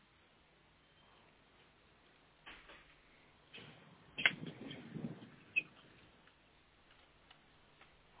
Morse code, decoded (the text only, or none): none